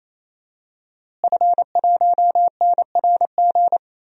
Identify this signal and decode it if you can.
Morse code, heard as a keyed tone, about 28 wpm, 700 Hz. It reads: F1NRG
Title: DXCC-100 Call Signs - 5 Characters - Repeated 3x - 28wpm